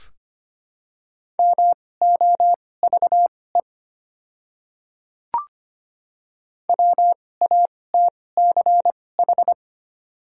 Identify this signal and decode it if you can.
Morse code, heard as a keyed tone, about 25 wpm, spaced as for 18 wpm, 700 Hz. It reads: MOVE WATCH
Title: 2x Character Spacing - Top 300 Words - 25wpm